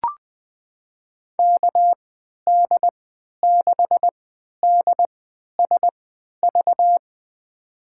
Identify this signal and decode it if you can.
Morse code, heard as a keyed tone, about 20 wpm, spaced as for 11 wpm, 700 Hz. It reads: KD6DSV
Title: USA Call Signs - 3x Character Spacing - 20wpm